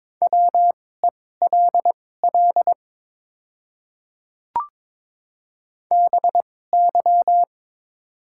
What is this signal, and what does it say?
Morse code, heard as a keyed tone, about 22 wpm, spaced as for 16 wpm, 700 Hz. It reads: WELL BY
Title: 2x Character Spacing - Top 200 Words - 22wpm